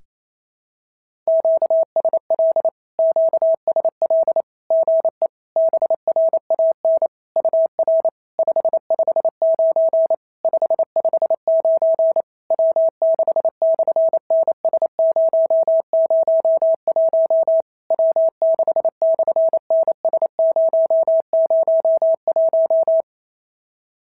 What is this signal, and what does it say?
Morse code, heard as a keyed tone, about 28 wpm, 650 Hz. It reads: QSL QSL GE BRAN UR 559 559 W6/NS001 W6/NS001
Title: SOTA QSO Practice 28wpm